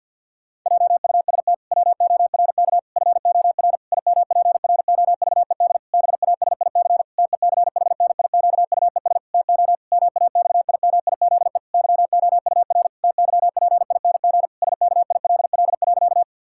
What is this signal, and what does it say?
Morse code, heard as a keyed tone, about 50 wpm, 700 Hz. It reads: JUST WORK FOR IMPROVED BASIC TECHNIQUES TO MAXIMIZE YOUR TYPING SKILL.
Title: Pangram Challenge v2 - 50wpm